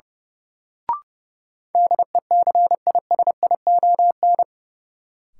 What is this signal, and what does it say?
Morse code, heard as a keyed tone, about 32 wpm, 700 Hz. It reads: DECISION
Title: Sentences from Top 1000 Words - New Word Drill - 30wpm